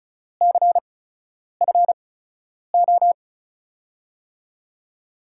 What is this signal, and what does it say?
Morse code, heard as a keyed tone, about 34 wpm, spaced as for 10 wpm, 700 Hz. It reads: CFO
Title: USA Call Signs - 5 Characters - 8x Character Spacing - 35wpm